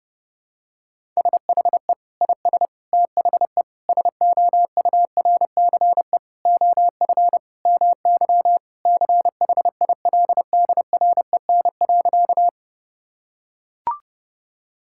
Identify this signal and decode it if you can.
Morse code, heard as a keyed tone, about 30 wpm, 700 Hz. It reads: SHE IS THE SOURCE OF MY CHILDREN.
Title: Sentences from Top 1000 Words - Review Part 5 - 30wpm